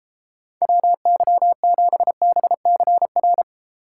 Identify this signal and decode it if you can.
Morse code, heard as a keyed tone, about 34 wpm, 700 Hz. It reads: WY7BCR
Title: Speed Racing - USA 2x3 Call Signs 28wpm